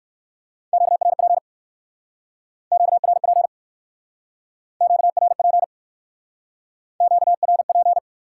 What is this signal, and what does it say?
Morse code, heard as a keyed tone, about 60 wpm, 700 Hz. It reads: QRP QRP QRP QRP